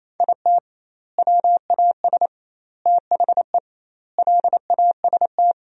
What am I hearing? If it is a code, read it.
Morse code, heard as a keyed tone, about 28 wpm, 700 Hz. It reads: IT WAS THE LAST
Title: 2x Word-Spacing - Sentences from Top 300 Words - 28wpm